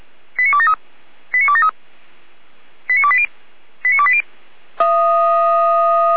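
Five-tone selective call (ZVEI), heard as quick runs of five tones, tones of about 70 ms. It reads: 89273 89273 89280 89280
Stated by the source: ZVEI five-tone selective call received by SDR, from the feuerwehr-orlamuende.org fire brigade server, address 89001.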